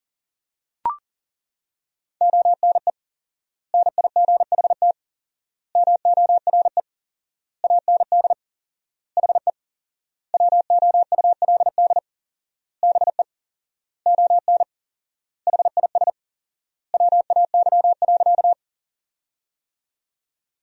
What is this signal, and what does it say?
Morse code, heard as a keyed tone, about 37 wpm, 700 Hz. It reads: ONE NIGHT MORE AND HE WOULD BE ON HIS WAY.
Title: 4x Word-Spacing - Sentences from Top 300 Words - 40wpm